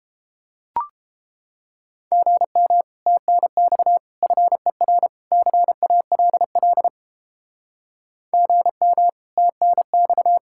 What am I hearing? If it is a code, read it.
Morse code, heard as a keyed tone, about 32 wpm, 700 Hz. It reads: GM TNX FER CALL GM TNX